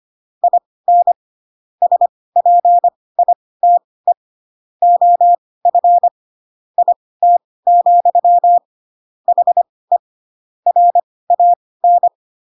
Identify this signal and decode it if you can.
Morse code, heard as a keyed tone, about 25 wpm, spaced as for 18 wpm, 700 Hz. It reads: IN SPITE OF IT, HE RAN
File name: Pinocchio / Pinocchio ch6 12 25 ik1hge